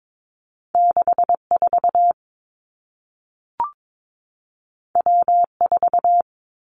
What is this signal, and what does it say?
Morse code, heard as a keyed tone, about 22 wpm, 700 Hz. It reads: 6 4 W 4
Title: Two Challenging Letter-Number Combinations 22wpm